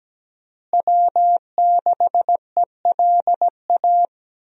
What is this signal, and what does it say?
Morse code, heard as a keyed tone, about 17 wpm, 700 Hz. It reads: W6ELA
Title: USA Call Signs - 5 Characters - Repeated 3x - 17wpm